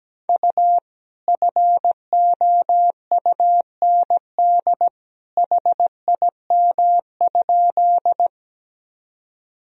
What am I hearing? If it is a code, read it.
Morse code, heard as a keyed tone, about 17 wpm, 700 Hz. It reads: U FOUND HIM?
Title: Sentences from Top 300 Words and Abbreviations 17wpm